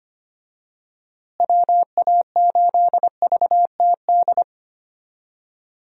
Morse code, heard as a keyed tone, about 25 wpm, 700 Hz. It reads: WA8VTD
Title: USA Call Signs - 6 Characters - Repeated 2x - 25wpm